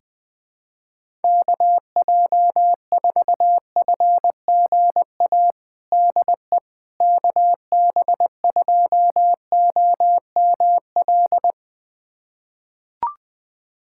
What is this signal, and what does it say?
Morse code, heard as a keyed tone, about 20 wpm, 700 Hz. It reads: KJ4FGA DE KB2OML
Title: USA Callsign DE Callsign - 6 Characters - 20wpm